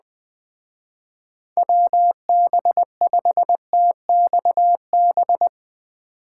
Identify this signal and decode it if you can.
Morse code, heard as a keyed tone, about 20 wpm, 700 Hz. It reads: WB5TXB